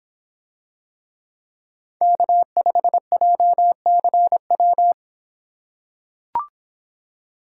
Morse code, heard as a keyed tone, about 24 wpm, 700 Hz. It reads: K5JCW